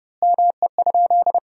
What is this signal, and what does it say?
Morse code, heard as a keyed tone, about 32 wpm, 700 Hz. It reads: ME?
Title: 3x Word-Spacing - Sentences from Top 500 Words - 30wpm